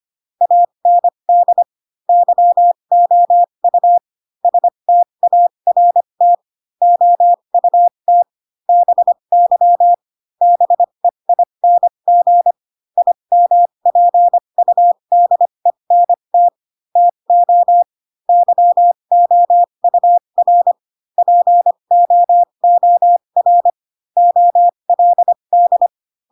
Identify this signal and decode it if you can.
Morse code, heard as a keyed tone, about 25 wpm, spaced as for 22 wpm, 700 Hz. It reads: AND YOU START OUT BY BEING IMPUDENT TO YOUR POOR OLD
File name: Pinocchio / Pinocchio ch3 18 25 ik1hge